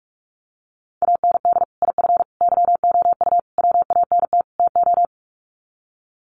Morse code, heard as a keyed tone, about 44 wpm, 700 Hz. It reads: AND IF YOU WANT TO
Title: Sets of 5 Words Encore 45wpm